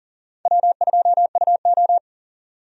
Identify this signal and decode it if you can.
Morse code, heard as a keyed tone, about 40 wpm, 700 Hz. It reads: W2UO
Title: USA Call Signs - 4 Characters - 40wpm